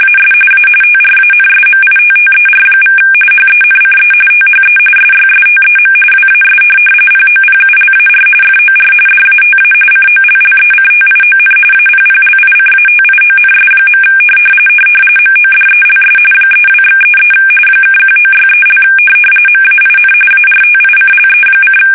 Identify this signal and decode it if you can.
RTTY, about 45.45 baud. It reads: AS A TRANSMITTER= HENCE THE SYSTEM WAS POPULAR FOR DATA COLLECTION APPLICATIONS. MODELS 32/33 (1963) MAIN ARTICLE:
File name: fsk850 / teletype-corp